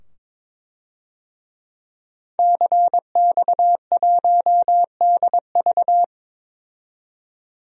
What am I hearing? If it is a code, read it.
Morse code, heard as a keyed tone, about 22 wpm, 700 Hz. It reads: CX1DV
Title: DXCC-100 Call Signs - 5 Characters - 22wpm